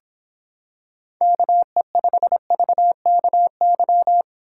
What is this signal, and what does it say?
Morse code, heard as a keyed tone, about 26 wpm, 700 Hz. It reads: KE5VKY